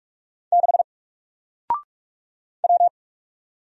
Morse code, heard as a keyed tone, about 44 wpm, 700 Hz. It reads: 6 W
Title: Single Letter-Number Alt 45wpm